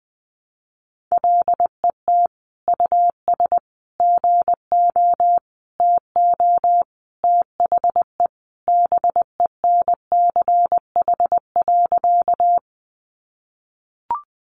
Morse code, heard as a keyed tone, about 20 wpm, 700 Hz. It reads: LET US GO TO THE BENCH.
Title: Sentences from Top 1400 Words 20wpm